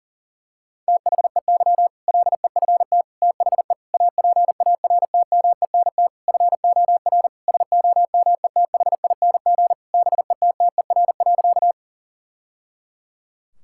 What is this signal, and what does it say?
Morse code, heard as a keyed tone, about 37 wpm, 700 Hz. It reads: THEY LEFT THE APARTMENT FOR SOMETHING BETTER.
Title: Sentences from Top 1000 Words - Review Part 6 - 40wpm